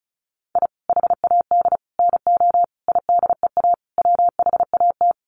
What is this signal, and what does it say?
Morse code, heard as a keyed tone, about 34 wpm, 700 Hz. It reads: I HAD NO IDEA WHAT
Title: Sets of 5 Words Encore 35wpm